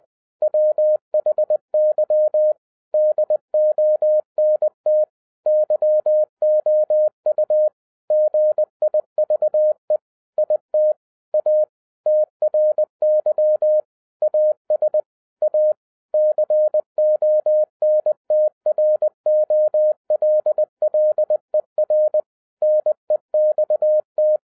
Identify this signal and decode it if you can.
Morse code, heard as a keyed tone, about 20 wpm, 610 Hz. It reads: WHY DONT YOU GIVE IT A TRY AS A CONTROLLER NEXT
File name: 2012-01-mp3 / n16-THROUGH EIGHT MONTHS-20wpm